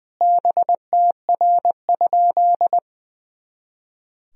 Morse code, heard as a keyed tone, about 20 wpm, 700 Hz. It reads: BTR?
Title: Sentences from Top 400 Words and Abbreviations 20wpm